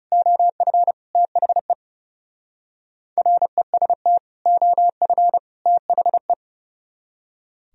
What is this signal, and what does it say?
Morse code, heard as a keyed tone, about 38 wpm, 700 Hz. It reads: OF THE REST OF THE